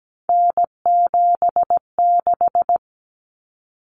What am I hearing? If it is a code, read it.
Morse code, heard as a keyed tone, about 17 wpm, 700 Hz. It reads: N76
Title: Common Cut Numbers - 3-Digit 17wpm